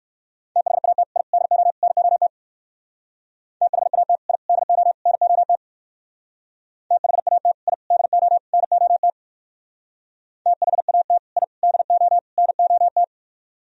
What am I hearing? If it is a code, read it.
Morse code, heard as a keyed tone, about 58 wpm, 700 Hz. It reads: THAT I DO NOT THAT I DO NOT THAT I DO NOT THAT I DO NOT